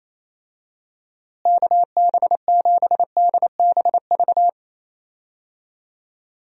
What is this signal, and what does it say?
Morse code, heard as a keyed tone, about 28 wpm, 700 Hz. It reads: KB7DBV
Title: USA Call Signs - 6 Characters - Repeated 2x - 28wpm